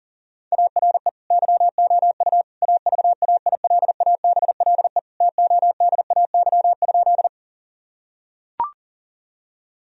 Morse code, heard as a keyed tone, about 40 wpm, 700 Hz. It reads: ARE YOU AVAILABLE TODAY?
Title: Sentences from Top 1000 Words - Review Part 3 - 40wpm